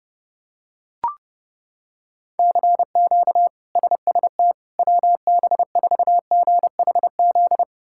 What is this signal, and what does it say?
Morse code, heard as a keyed tone, about 31 wpm, 700 Hz. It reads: CQ SST WB4GHZ
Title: K1USN SST QSO Practice - 30wpm